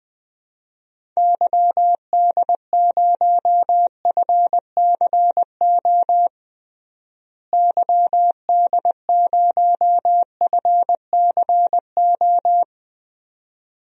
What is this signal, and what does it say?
Morse code, heard as a keyed tone, about 21 wpm, 700 Hz. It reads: YD0FCO YD0FCO
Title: DXCC-100 Call Signs - 6 Characters - Repeated 2x - 20wpm